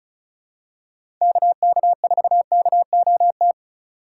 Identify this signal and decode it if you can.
Morse code, heard as a keyed tone, about 36 wpm, 700 Hz. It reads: KK4KOT
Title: USA Call Signs - 6 Characters - 4x Character Spacing - 35wpm